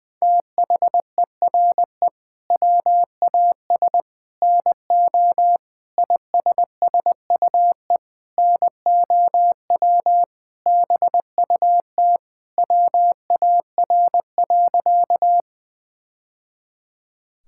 Morse code, heard as a keyed tone, about 20 wpm, 700 Hz. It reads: THERE WAS NO ISSUE NOW BUT WAR.
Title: Sentences from Top 1000 Words - Review Part 6 - 20wpm